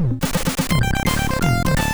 There is no code.